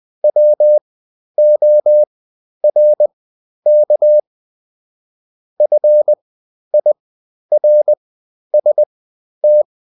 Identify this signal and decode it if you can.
Morse code, heard as a keyed tone, about 20 wpm, spaced as for 11 wpm, 600 Hz. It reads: WORK FIRST
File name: Top-100-Words-10